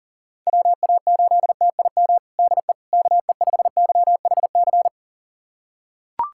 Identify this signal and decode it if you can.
Morse code, heard as a keyed tone, about 40 wpm, 700 Hz. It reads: WA8TIM DE KE5YHC
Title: USA Callsign DE Callsign 40wpm